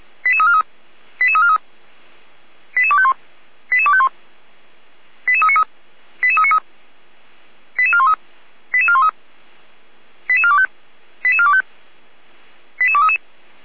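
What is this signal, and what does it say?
Five-tone selective call (ZVEI), heard as quick runs of five tones, tones of about 80 ms. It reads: 88343 88343 88251 88251 88393 88393 88413 88413 88425 88425 88233
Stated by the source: ZVEI five-tone selective call, received on an SDR.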